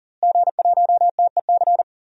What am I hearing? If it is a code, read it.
Morse code, heard as a keyed tone, about 37 wpm, 700 Hz. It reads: G1TEC
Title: DXCC-100 Call Signs - 5 Characters - Repeated 3x - 40wpm